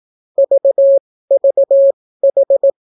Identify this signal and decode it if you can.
Morse code, heard as a keyed tone, about 18 wpm, spaced as for 14 wpm, 550 Hz. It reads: VVH